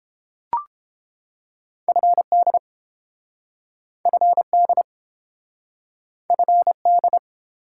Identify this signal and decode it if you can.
Morse code, heard as a keyed tone, about 30 wpm, 700 Hz. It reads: FD FD FD